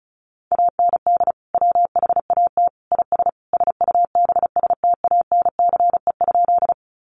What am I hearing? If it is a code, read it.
Morse code, heard as a keyed tone, about 34 wpm, 700 Hz. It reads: AND WHAT IS SUBSTANCE?
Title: Sentences from Top 1500 Words 35wpm